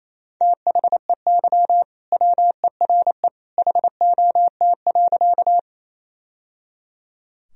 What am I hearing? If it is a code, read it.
Morse code, heard as a keyed tone, about 28 wpm, 700 Hz. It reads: THEY WERE HOT.
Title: Sentences from Top 1000 Words - Review Part 3 - 28wpm